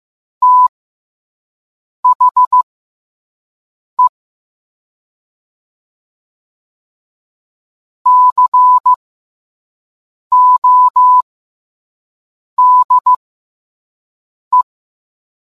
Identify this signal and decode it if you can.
Morse code, heard as a keyed tone, about 15 wpm, spaced as for 5 wpm, 1000 Hz. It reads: THE CODE